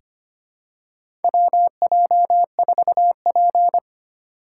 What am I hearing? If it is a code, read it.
Morse code, heard as a keyed tone, about 25 wpm, 700 Hz. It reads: WJ4P
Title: USA Call Signs - 6x Character Spacing - 25wpm